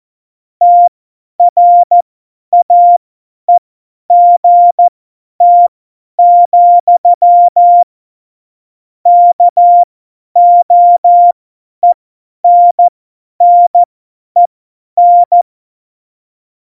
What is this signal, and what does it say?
Morse code, heard as a keyed tone, about 14 wpm, spaced as for 10 wpm, 700 Hz. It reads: TRAEGT, KOENNEN